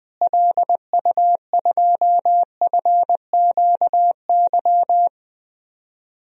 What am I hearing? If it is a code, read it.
Morse code, heard as a keyed tone, about 20 wpm, 700 Hz. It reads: LU2FQY
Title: DXCC-100 Call Signs - 6 Characters - Repeated 2x - 20wpm